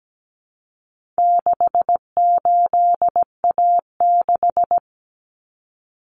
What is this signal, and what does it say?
Morse code, heard as a keyed tone, about 17 wpm, 700 Hz. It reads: 68A6
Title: Common Cut Numbers - 4-Digit 17wpm